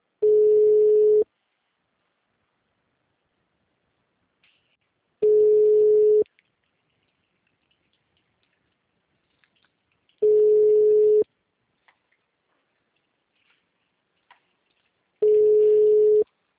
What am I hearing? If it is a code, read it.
Ringback tone (Europe, 425 Hz), with speech.